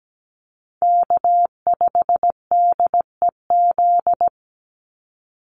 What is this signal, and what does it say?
Morse code, heard as a keyed tone, about 17 wpm, 700 Hz. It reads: K5DEZ
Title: NA SOTA Chaser Callsigns 17wpm